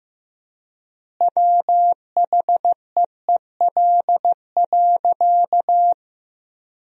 Morse code, heard as a keyed tone, about 15 wpm, 700 Hz. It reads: WHEEL.
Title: 3x Word-Spacing - Sentences from Top 500 Words - 15wpm